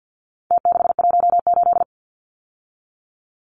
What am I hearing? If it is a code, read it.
Morse code, heard as a keyed tone, about 49 wpm, 700 Hz. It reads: T618
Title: Numbers - Contest Style - 4-Digit 50wpm